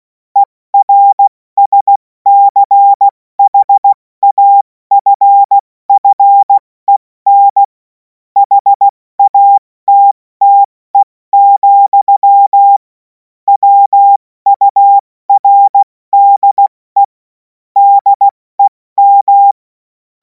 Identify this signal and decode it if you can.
Morse code, heard as a keyed tone, about 16 wpm, spaced as for 14 wpm, 800 Hz. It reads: ERSCHAFFEN HATTE, WURDE DEM
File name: Morsekurs / Witze / Witz 24. Februar 2026 70 F80